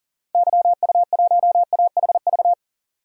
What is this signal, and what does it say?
Morse code, heard as a keyed tone, about 40 wpm, 700 Hz. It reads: YU1AHV
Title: DXCC-100 Call Signs - 6 Characters - 40wpm